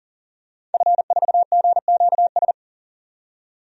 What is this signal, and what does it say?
Morse code, heard as a keyed tone, about 40 wpm, 700 Hz. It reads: F4GQS
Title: DXCC-100 Call Signs - 5 Characters - 40wpm